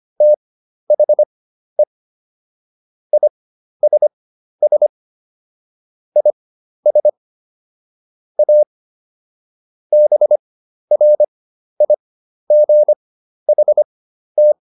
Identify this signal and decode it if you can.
Morse code, heard as a keyed tone, about 25 wpm, spaced as for 12 wpm, 600 Hz. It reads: THE ISS IS A BRIGHT